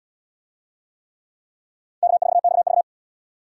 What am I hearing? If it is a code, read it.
Morse code, heard as a keyed tone, about 72 wpm, 700 Hz. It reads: K4XV